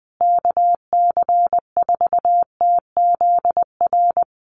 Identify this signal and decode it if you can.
Morse code, heard as a keyed tone, about 20 wpm, 700 Hz. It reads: KC4TZR
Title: USA Call Signs Repeated 3x 20wpm